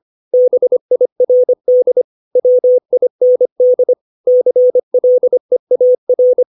Morse code, heard as a keyed tone, about 25 wpm, 500 Hz. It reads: BIRD WIND CLEAR
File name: 632 words / 632 Words 25WPM